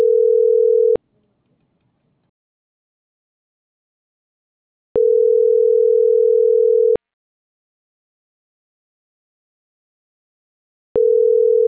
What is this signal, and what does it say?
Ringback tone (North America), 440 + 480 Hz.